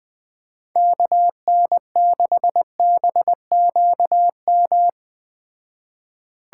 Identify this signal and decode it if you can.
Morse code, heard as a keyed tone, about 20 wpm, 700 Hz. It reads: KN6BQM
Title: Speed Racing - USA 2x3 Call Signs 17wpm